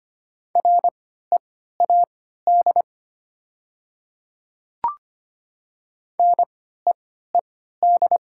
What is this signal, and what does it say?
Morse code, heard as a keyed tone, about 25 wpm, spaced as for 14 wpm, 700 Hz. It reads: READ NEED
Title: 3x Character Spacing - Top 300 Words - 25wpm